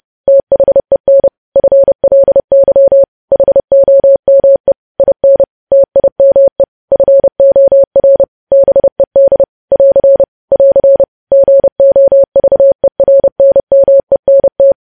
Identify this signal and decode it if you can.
Morse code, heard as a keyed tone, about 30 wpm, 550 Hz. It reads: THEN FLY HOME IN TIME FOR BED <AR> <AR> GOVERNMENT
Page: news headlines in morse code at 30 wpm